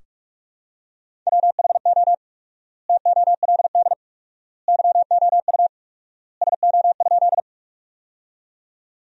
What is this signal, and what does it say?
Morse code, heard as a keyed tone, about 46 wpm, 700 Hz. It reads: WHO TOLD YOU SO?
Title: 4x Word-Spacing - Sentences from Top 400 Words - 45wpm